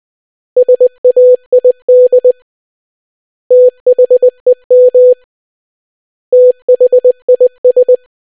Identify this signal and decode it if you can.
Morse code, heard as a keyed tone, about 20 wpm, 500 Hz. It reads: SAID THEM THIS